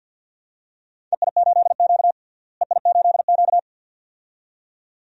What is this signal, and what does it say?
Morse code, heard as a keyed tone, about 47 wpm, 700 Hz. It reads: EI8Q EI8Q
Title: DXCC-100 Call Signs - 4 Characters - Repeated 2x - 50wpm